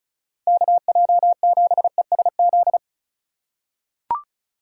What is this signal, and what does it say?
Morse code, heard as a keyed tone, about 36 wpm, 700 Hz. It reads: KJ7ESZ
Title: USA Call Signs - 6 Characters - 2x Character Spacing - 35wpm